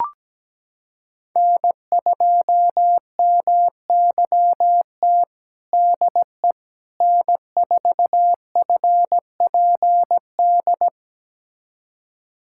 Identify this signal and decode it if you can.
Morse code, heard as a keyed tone, about 17 wpm, 700 Hz. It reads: N2MYT DE N4FPD
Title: USA Callsign DE Callsign - 5 Characters - 17wpm